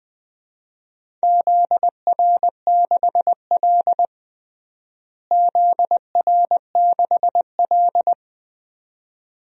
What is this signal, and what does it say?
Morse code, heard as a keyed tone, about 21 wpm, 700 Hz. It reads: ZR6L ZR6L
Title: DXCC-100 Call Signs - 4 Characters - Repeated 2x - 20wpm